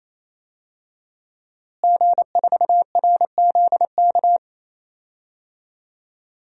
Morse code, heard as a keyed tone, about 28 wpm, 700 Hz. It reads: G4RZK
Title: DXCC-100 Call Signs - 5 Characters - Repeated 3x - 28wpm